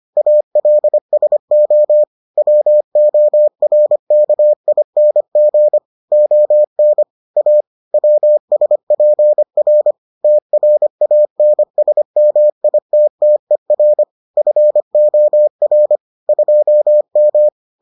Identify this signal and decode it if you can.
Morse code, heard as a keyed tone, about 25 wpm, 600 Hz. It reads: ALSO WORKING ON A WSPR TRANSMITTER FOR 2M